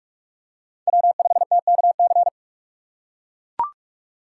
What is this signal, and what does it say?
Morse code, heard as a keyed tone, about 43 wpm, 700 Hz. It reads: W5TKC
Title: USA Call Signs - 5 Characters - 6x Character Spacing - 45wpm